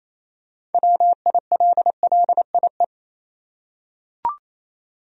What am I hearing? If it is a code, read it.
Morse code, heard as a keyed tone, about 28 wpm, 700 Hz. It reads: WILLIE
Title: USA Common Names - 28wpm